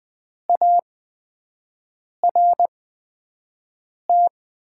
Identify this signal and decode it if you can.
Morse code, heard as a keyed tone, about 20 wpm, spaced as for 5 wpm, 700 Hz. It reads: ART
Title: 8x Character Spacing - Top 500 Words - 20wpm